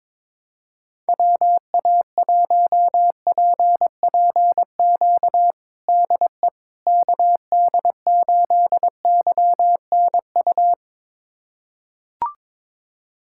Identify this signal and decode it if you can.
Morse code, heard as a keyed tone, about 22 wpm, 700 Hz. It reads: WA1PPQ DE KD8YNU
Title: USA Callsign DE Callsign - 6 Characters - 22wpm